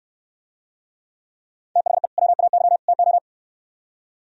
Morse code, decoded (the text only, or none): THE WAY TO